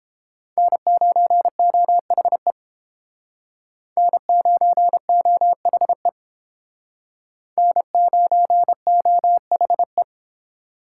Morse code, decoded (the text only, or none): N9OHE N9OHE N9OHE